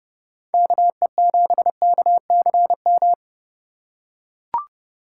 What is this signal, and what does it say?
Morse code, heard as a keyed tone, about 30 wpm, 700 Hz. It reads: KE7KCM
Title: USA Call Signs - 6 Characters - 3x Character Spacing - 30wpm